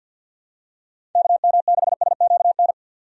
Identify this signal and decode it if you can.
Morse code, heard as a keyed tone, about 47 wpm, 700 Hz. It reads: KM6SQN
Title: USA Call Signs - 6 Characters - 50wpm